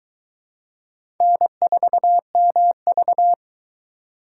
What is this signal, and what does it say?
Morse code, heard as a keyed tone, about 22 wpm, 700 Hz. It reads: N4MV